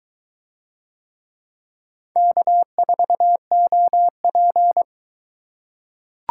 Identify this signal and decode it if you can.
Morse code, heard as a keyed tone, about 21 wpm, 700 Hz. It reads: K4OP